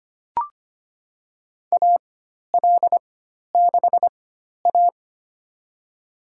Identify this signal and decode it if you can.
Morse code, heard as a keyed tone, about 25 wpm, spaced as for 12 wpm, 700 Hz. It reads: AL6A
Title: USA Call Signs - 4x Character Spacing - 25wpm